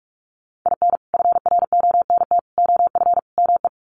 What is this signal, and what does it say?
Morse code, heard as a keyed tone, about 44 wpm, 700 Hz. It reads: IN FRONT OF ME
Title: Sets of 4 Words 45wpm